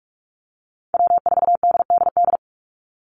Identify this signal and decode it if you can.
Morse code, heard as a keyed tone, about 44 wpm, 700 Hz. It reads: W4DDD
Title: Call Signs Repeated 2x - 45wpm